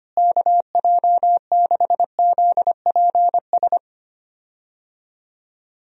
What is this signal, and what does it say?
Morse code, heard as a keyed tone, about 25 wpm, 700 Hz. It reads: KJ6ZPS
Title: USA Call Signs - 6 Characters - Repeated 2x - 25wpm